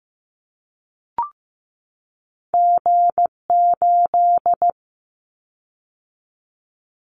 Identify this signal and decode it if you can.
Morse code, heard as a keyed tone, about 15 wpm, 700 Hz. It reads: G8
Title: Callsign Prefixes for Top 25 DXCC - 15wpm